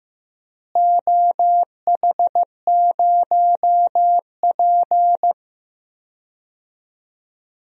Morse code, heard as a keyed tone, about 15 wpm, 700 Hz. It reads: OH0P
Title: DXCC-100 Call Signs - 4 Characters - Repeated 2x - 15wpm